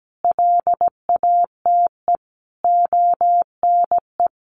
Morse code, heard as a keyed tone, about 17 wpm, 700 Hz. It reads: LATE ONE
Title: Sentences from Top 1500 Words 17wpm